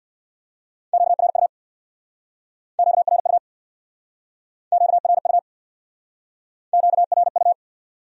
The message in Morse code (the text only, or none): QRU QRU QRU QRU